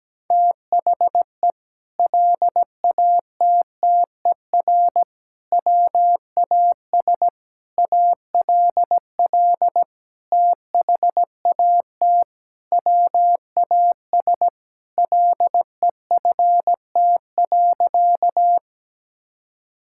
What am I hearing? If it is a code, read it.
Morse code, heard as a keyed tone, about 17 wpm, 700 Hz. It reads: THE LATTER WAS ALL THAT WAS LEFT.
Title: Sentences from Top 1600 Words 17wpm